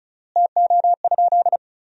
Morse code, heard as a keyed tone, about 36 wpm, 700 Hz. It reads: TO?
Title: 3x Word-Spacing - Sentences from Top 100 Words - 35wpm